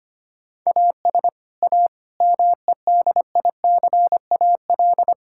Morse code, heard as a keyed tone, about 25 wpm, 700 Hz. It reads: AS A MEDICAL
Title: Sentences from Top 1000 Words - Review Part 5 - 25wpm